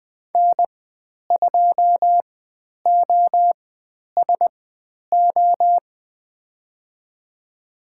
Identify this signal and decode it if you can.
Morse code, heard as a keyed tone, about 20 wpm, spaced as for 10 wpm, 700 Hz. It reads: N2OSO
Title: 095 Lesson 16g - Letter c - Call Signs 20wpm